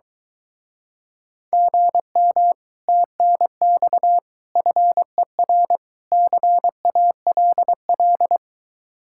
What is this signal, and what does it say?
Morse code, heard as a keyed tone, about 22 wpm, 700 Hz. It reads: GM TNX FER CALL